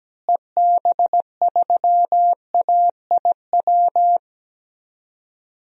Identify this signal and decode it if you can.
Morse code, heard as a keyed tone, about 17 wpm, 700 Hz. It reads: EB3AIW
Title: DXCC-100 Call Signs - 6 Characters - Repeated 3x - 17wpm